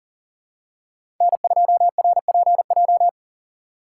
Morse code, heard as a keyed tone, about 40 wpm, 700 Hz. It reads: N2RPJ
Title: USA Call Signs - 2x Character Spacing - 40wpm